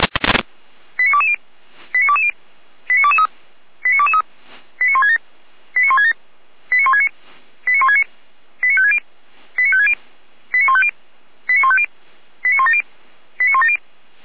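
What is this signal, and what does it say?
Five-tone selective call (ZVEI), heard as quick runs of five tones, tones of about 70 ms. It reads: 89220 89220 89223 89223 89167 89167 89169 89169 89560 89560 89150 89150 89170 89170
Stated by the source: ZVEI five-tone selective call 89001 received by SDR, logged on the Feuerwehr Orlamünde fire brigade server.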